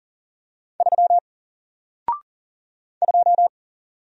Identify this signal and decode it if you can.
Morse code, heard as a keyed tone, about 40 wpm, 700 Hz. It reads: 3 2
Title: Single Character (Letter:Number:Punctuation:Prosign) 40wpm